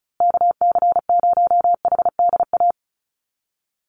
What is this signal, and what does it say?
Morse code, heard as a keyed tone, about 34 wpm, 700 Hz. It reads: KC0HDA
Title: USA Call Signs Repeated 2x 35wpm